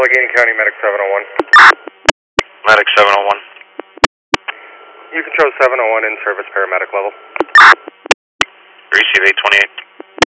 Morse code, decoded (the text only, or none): none